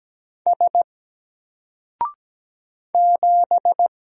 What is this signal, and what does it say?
Morse code, heard as a keyed tone, about 17 wpm, 700 Hz. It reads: S 7
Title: Single Character (Letter:Number:Punctuation:Prosign) 17wpm